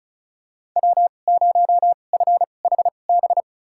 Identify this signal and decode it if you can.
Morse code, heard as a keyed tone, about 35 wpm, spaced as for 25 wpm, 700 Hz. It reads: W0FHB
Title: USA Call Signs - 5 Characters - 2x Character Spacing - 35wpm